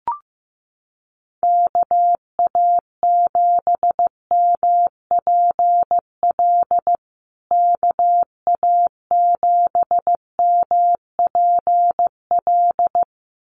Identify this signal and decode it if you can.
Morse code, heard as a keyed tone, about 15 wpm, 700 Hz. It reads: KA7MPL KA7MPL
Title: USA Call Signs Repeated 2x 15wpm